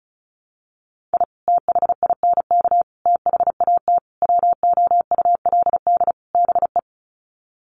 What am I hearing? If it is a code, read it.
Morse code, heard as a keyed tone, about 34 wpm, 700 Hz. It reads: I THINK THAT WOULD BE
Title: Sets of 5 Words Encore2 35wpm